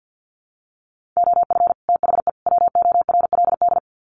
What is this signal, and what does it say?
Morse code, heard as a keyed tone, about 49 wpm, 700 Hz. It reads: OF THE WORLD
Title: Sets of 3 Words 50wpm